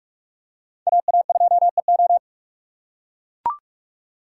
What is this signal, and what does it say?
Morse code, heard as a keyed tone, about 44 wpm, 700 Hz. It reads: AA2EO